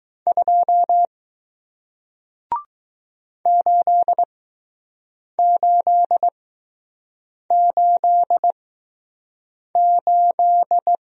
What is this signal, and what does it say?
Morse code, heard as a keyed tone, about 20 wpm, spaced as for 6 wpm, 700 Hz. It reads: 2 8888